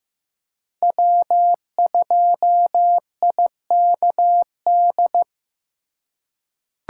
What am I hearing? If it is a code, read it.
Morse code, heard as a keyed tone, about 15 wpm, 700 Hz. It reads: W2IKD
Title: USA Call Signs - 5 Characters - 4x Character Spacing - 15wpm